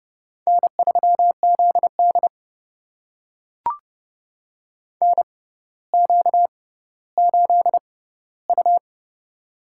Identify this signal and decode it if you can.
Morse code, heard as a keyed tone, about 30 wpm, spaced as for 15 wpm, 700 Hz. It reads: N3ZD NQ8U
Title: USA Call Signs - 4 Characters - 6x Character Spacing - 30wpm